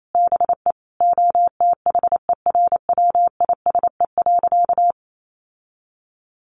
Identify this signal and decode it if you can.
Morse code, heard as a keyed tone, about 28 wpm, 700 Hz. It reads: BE OTHERWISE.
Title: Sentences from Top 1200 Words 28wpm